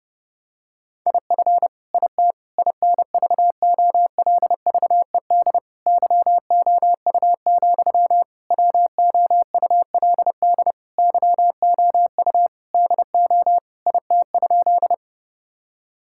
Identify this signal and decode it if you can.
Morse code, heard as a keyed tone, about 31 wpm, 700 Hz. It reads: IF IT INVOLVED YOU, WOULD YOU DO IT?
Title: Sentences from Top 1000 Words - Review Part 5 - 30wpm